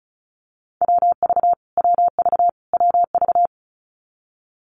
Morse code, heard as a keyed tone, about 34 wpm, 700 Hz. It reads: WV WV WV
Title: US States Repeated 3x 35wpm